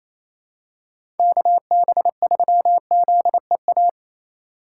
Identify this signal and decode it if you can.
Morse code, heard as a keyed tone, about 28 wpm, 700 Hz. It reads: KB3ZEA
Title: USA Call Signs - 6 Characters - 4x Character Spacing - 28wpm